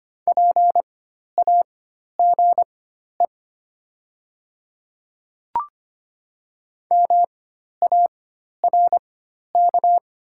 Morse code, heard as a keyed tone, about 25 wpm, spaced as for 12 wpm, 700 Hz. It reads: PAGE MARK
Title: 4x Character Spacing - Top 300 Words - 25wpm